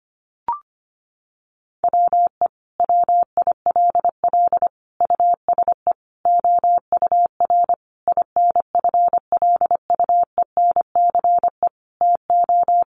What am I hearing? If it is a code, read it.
Morse code, heard as a keyed tone, about 25 wpm, 700 Hz. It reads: WE WILL USE OUR INFLUENCE TO
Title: Sentences from Top 1100 Words 25wpm